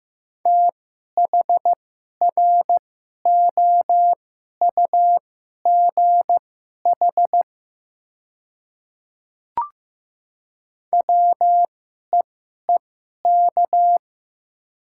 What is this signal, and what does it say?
Morse code, heard as a keyed tone, about 15 wpm, spaced as for 11 wpm, 700 Hz. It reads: THROUGH WEEK
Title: 2x Character Spacing - Top 500 Words - 15wpm